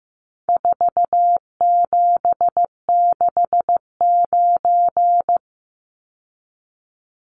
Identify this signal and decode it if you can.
Morse code, heard as a keyed tone, about 15 wpm, 700 Hz. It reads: 4769
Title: Numbers - 4-Digit 15wpm